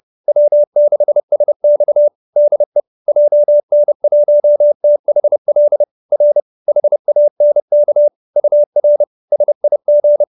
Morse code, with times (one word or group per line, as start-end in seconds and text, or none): 0.3-2.1 W6SX
2.4-2.8 DE
3.1-5.8 JN1THL
6.1-6.4 R
6.7-8.1 HANK
8.4-9.0 UR
9.3-10.2 SIG